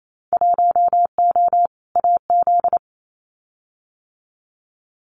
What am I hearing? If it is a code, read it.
Morse code, heard as a keyed tone, about 28 wpm, 700 Hz. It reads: 1O AZ
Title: Winter Field Day Exchanges 28wpm